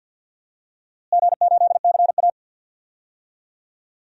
Morse code, heard as a keyed tone, about 51 wpm, 700 Hz. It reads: G8CA
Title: DXCC-100 Call Signs - 4 Characters - Repeated 2x - 50wpm